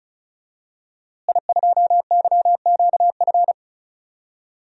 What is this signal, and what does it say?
Morse code, heard as a keyed tone, about 36 wpm, 700 Hz. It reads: I2YQF
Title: DXCC-100 Call Signs - 5 Characters - Repeated 3x - 35wpm